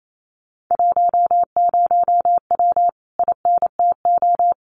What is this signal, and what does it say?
Morse code, heard as a keyed tone, about 28 wpm, 700 Hz. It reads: 10W INTO
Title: Common QSO Phrases Encore 28wpm